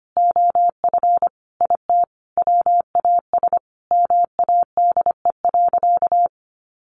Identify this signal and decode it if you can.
Morse code, heard as a keyed tone, about 25 wpm, 700 Hz. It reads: OF IT WAS MADE.
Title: Sentences from Top 1400 Words 25wpm